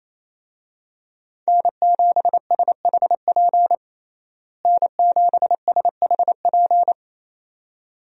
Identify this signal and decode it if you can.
Morse code, heard as a keyed tone, about 28 wpm, 700 Hz. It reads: N7SHP N7SHP
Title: USA Call Signs - 5 Characters - Repeated 2x - 28wpm